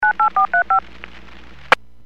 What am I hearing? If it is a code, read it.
DTMF, heard as telephone touch tones, tones of about 110 ms. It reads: 98435